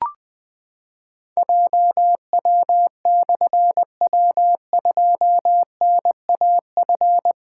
Morse code, heard as a keyed tone, about 21 wpm, 700 Hz. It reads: JW/W2NAF
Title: Difficult Call Signs 20wpm